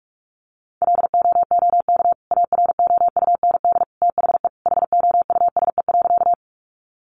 Morse code, heard as a keyed tone, about 44 wpm, 700 Hz. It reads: LOOK AROUND THE HOUSE.
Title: Sentences from Top 600 Words 45wpm